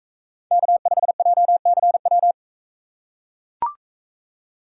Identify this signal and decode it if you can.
Morse code, heard as a keyed tone, about 41 wpm, 700 Hz. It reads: K5JCW